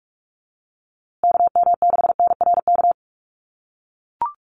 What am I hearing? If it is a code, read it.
Morse code, heard as a keyed tone, about 44 wpm, 700 Hz. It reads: KM6NRK
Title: USA Call Signs Repeated 2x 45wpm